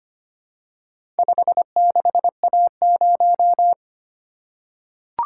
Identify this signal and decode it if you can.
Morse code, heard as a keyed tone, about 25 wpm, 700 Hz. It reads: 56A0
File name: practice / 25wpm / Four Letter-Number Combinations - 25wpm